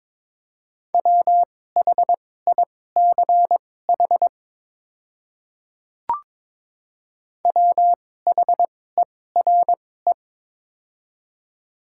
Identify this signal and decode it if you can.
Morse code, heard as a keyed tone, about 22 wpm, spaced as for 16 wpm, 700 Hz. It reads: WHICH WHERE
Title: 2x Character Spacing - Top 500 Words - 22wpm